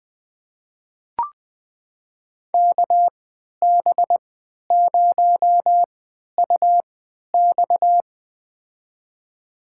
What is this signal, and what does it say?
Morse code, heard as a keyed tone, about 20 wpm, spaced as for 11 wpm, 700 Hz. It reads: KB0UX
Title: USA Call Signs - 5 Characters - 3x Character Spacing - 20wpm